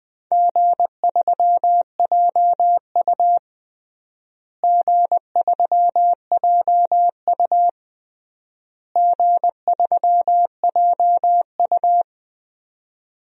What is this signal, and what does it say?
Morse code, heard as a keyed tone, about 20 wpm, 700 Hz. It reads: G3JU G3JU G3JU
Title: DXCC-100 Call Signs - 4 Characters - Repeated 3x - 20wpm